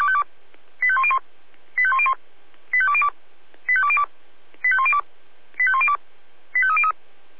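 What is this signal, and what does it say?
Five-tone selective call (ZVEI), heard as quick runs of five tones, tones of about 70 ms. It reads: 85101 85101 85202 85202 85102 85102 85303
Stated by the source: ZVEI five-tone selective call received via SDR.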